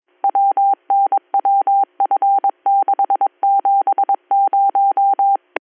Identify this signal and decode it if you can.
Morse code, heard as a keyed tone, about 22 wpm, 790 Hz. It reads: WNWF670